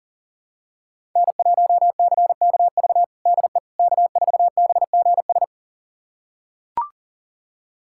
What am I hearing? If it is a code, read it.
Morse code, heard as a keyed tone, about 37 wpm, 700 Hz. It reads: N1CKV DE K4BGS
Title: USA Callsign DE Callsign - 5 Characters - 40wpm